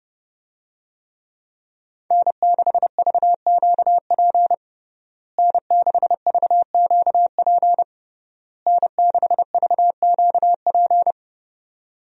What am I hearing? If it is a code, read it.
Morse code, heard as a keyed tone, about 30 wpm, 700 Hz. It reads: N6VQP N6VQP N6VQP
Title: USA Call Signs - 5 Characters - Repeated 3x - 30wpm